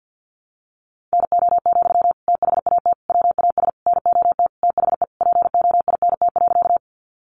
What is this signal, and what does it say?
Morse code, heard as a keyed tone, about 50 wpm, 700 Hz. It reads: NO, THAT WAS NOT THE POINT.
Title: Sentences from Top 600 Words 50wpm